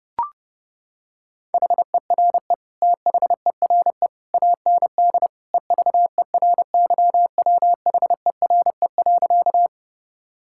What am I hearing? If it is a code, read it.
Morse code, heard as a keyed tone, about 30 wpm, 700 Hz. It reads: HERE THERE AND EVERYWHERE.
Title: Sentences from Top 1600 Words 30wpm